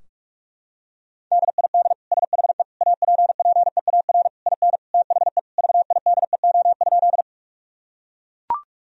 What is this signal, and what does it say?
Morse code, heard as a keyed tone, about 41 wpm, 700 Hz. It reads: DID SHE APPEAR IN THE VIDEO?
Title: Sentences from Top 1000 Words - Review Part 4 - 45wpm